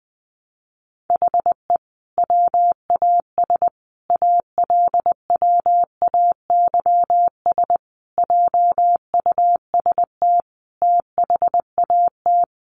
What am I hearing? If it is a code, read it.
Morse code, heard as a keyed tone, about 20 wpm, 700 Hz. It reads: HE WAS ALWAYS JUST THAT
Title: Sentences from Top 600 Words 20wpm